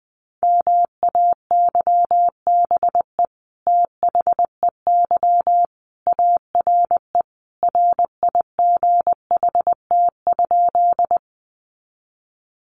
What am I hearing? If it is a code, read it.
Morse code, heard as a keyed tone, about 19 wpm, 700 Hz. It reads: MAYBE THEY ARE RIGHT?
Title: Sentences from Top 1000 Words 20wpm